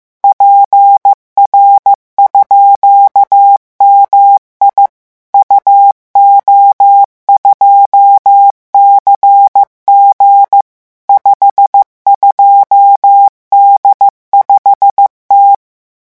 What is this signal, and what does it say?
Morse code, heard as a keyed tone, about 15 wpm, 800 Hz. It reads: PR_MI UO2CG 52D5T